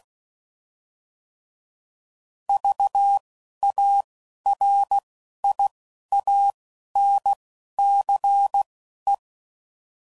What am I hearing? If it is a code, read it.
Morse code, heard as a keyed tone, about 16 wpm, spaced as for 11 wpm, 800 Hz. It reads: VARIANCE
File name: Prefixes / By First Letter / Vs